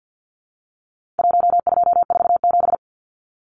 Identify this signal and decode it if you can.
Morse code, heard as a keyed tone, about 50 wpm, 700 Hz. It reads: 1247
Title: Numbers - Contest Style - 4-Digit 50wpm